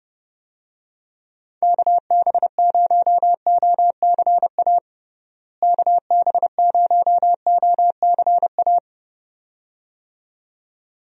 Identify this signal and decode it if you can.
Morse code, heard as a keyed tone, about 32 wpm, 700 Hz. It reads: KB0OCA KB0OCA
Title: USA Call Signs - 6 Characters - Repeated 2x - 30wpm